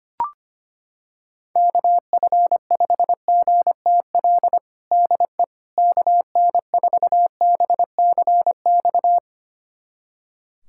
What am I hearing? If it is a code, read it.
Morse code, heard as a keyed tone, about 25 wpm, 700 Hz. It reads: KF5GTL DE KN4BCX
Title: USA Callsign DE Callsign 25wpm